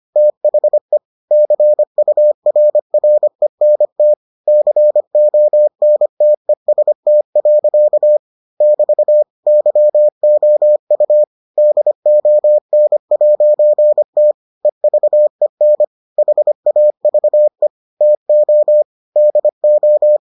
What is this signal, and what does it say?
Morse code, heard as a keyed tone, about 25 wpm, 600 Hz. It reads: THE CURRENT CONTEST. <BT> YOU DON'T EVEN HAVE TO DO